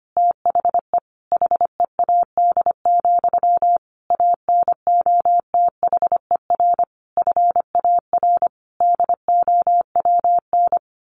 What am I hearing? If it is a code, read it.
Morse code, heard as a keyed tone, about 25 wpm, 700 Hz. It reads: THE HEAD, ANOTHER FAR DOWN
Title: A Sound Of Thunder 25wpm